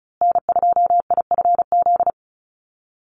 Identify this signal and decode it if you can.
Morse code, heard as a keyed tone, about 34 wpm, 700 Hz. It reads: N2IFZ
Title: USA Call Signs Repeated 3x 35wpm